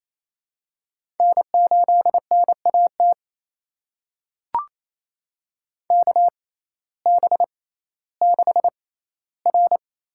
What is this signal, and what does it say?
Morse code, heard as a keyed tone, about 28 wpm, spaced as for 15 wpm, 700 Hz. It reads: N8NAT KB6R
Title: USA Call Signs - 5 Characters - 6x Character Spacing - 28wpm